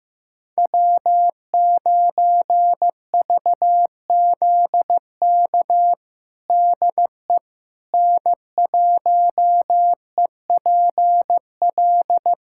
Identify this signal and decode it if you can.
Morse code, heard as a keyed tone, about 15 wpm, 700 Hz. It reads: W9VZK DE N1EPL
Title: USA Callsign DE Callsign - 5 Characters - 15wpm